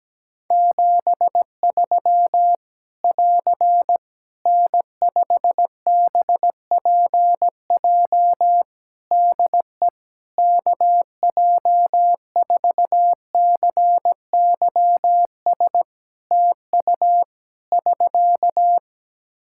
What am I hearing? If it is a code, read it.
Morse code, heard as a keyed tone, about 17 wpm, 700 Hz. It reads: 73 <AR> N5BPJ DE KJ4CYS TU <SK>
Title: Practice QSO - Exchange 4 - Ending - 17wpm